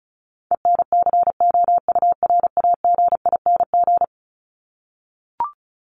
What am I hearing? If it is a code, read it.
Morse code, heard as a keyed tone, about 34 wpm, 700 Hz. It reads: ENCOURAGING
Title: Top 1751-2000 Words 35wpm